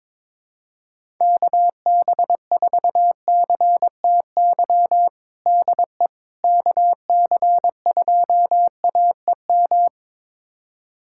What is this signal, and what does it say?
Morse code, heard as a keyed tone, about 22 wpm, 700 Hz. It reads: KB4CTY DE KC2AEM
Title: USA Callsign DE Callsign 22wpm